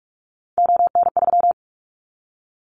Morse code, heard as a keyed tone, about 46 wpm, 700 Hz. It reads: ON3
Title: Callsign Prefixes for Top 25 DXCC - Repeated 3x - 45wpm